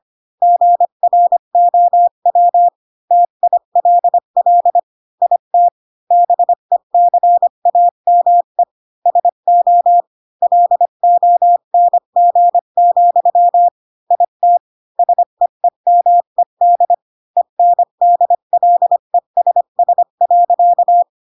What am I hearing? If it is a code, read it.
Morse code, heard as a keyed tone, about 25 wpm, 700 Hz. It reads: GROW TILL IT BECAME SO LONG, IT SEEMED ENDLESS.
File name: Pinocchio / Pinocchio ch3 20 25 ik1hge